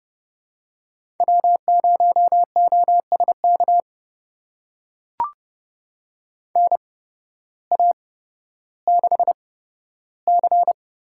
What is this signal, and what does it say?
Morse code, heard as a keyed tone, about 32 wpm, spaced as for 14 wpm, 700 Hz. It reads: W0OSK NA6C
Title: USA Call Signs - 8x Character Spacing - 30wpm